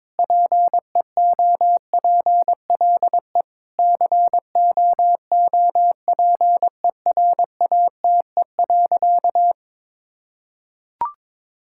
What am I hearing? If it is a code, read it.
Morse code, heard as a keyed tone, about 22 wpm, 700 Hz. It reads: PEOPLE COOPERATE.
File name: Difficult Sentences with Hints - 22wpm